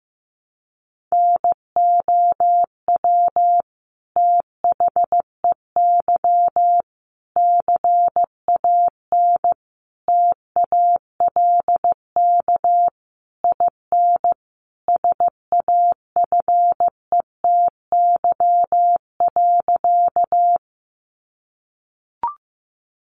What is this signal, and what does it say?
Morse code, heard as a keyed tone, about 15 wpm, 700 Hz. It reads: NOW THEY CAN TALK IN SAFETY.
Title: Sentences from Top 700 Words 15wpm